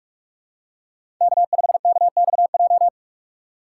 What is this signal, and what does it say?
Morse code, heard as a keyed tone, about 45 wpm, 700 Hz. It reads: K5KXJ
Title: K1USN SST QSO Practice - 45wpm